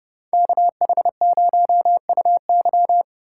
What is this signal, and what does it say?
Morse code, heard as a keyed tone, about 29 wpm, 700 Hz. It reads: KH0UY